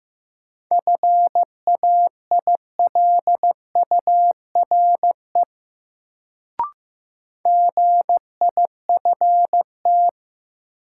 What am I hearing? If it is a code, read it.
Morse code, heard as a keyed tone, about 15 wpm, 700 Hz. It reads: FAILURE GIFT
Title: Sentences from Top 900 Words - New Word Drill - 15wpm